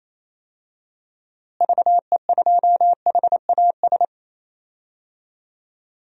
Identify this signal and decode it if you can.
Morse code, heard as a keyed tone, about 28 wpm, 700 Hz. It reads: VE2HAS
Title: DXCC-100 Call Signs - 6 Characters - Repeated 3x - 28wpm